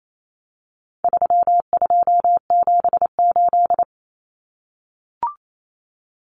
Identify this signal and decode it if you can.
Morse code, heard as a keyed tone, about 28 wpm, 700 Hz. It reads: 3278
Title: Common Cut Numbers - 4-Digit 28wpm